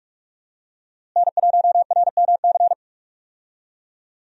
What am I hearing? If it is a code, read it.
Morse code, heard as a keyed tone, about 46 wpm, 700 Hz. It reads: N1RMC
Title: USA Call Signs - 5 Characters - Repeated 3x - 45wpm